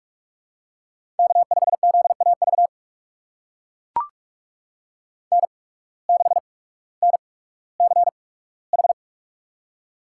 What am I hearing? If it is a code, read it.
Morse code, heard as a keyed tone, about 44 wpm, spaced as for 19 wpm, 690 Hz. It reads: K5ZAV N6NCH
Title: USA Call Signs - 5 Characters - 8x Character Spacing - 45wpm